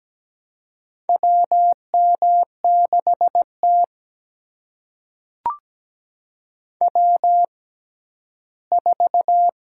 Morse code, heard as a keyed tone, about 17 wpm, spaced as for 11 wpm, 700 Hz. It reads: WM6T W4
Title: USA Call Signs - 6x Character Spacing - 17wpm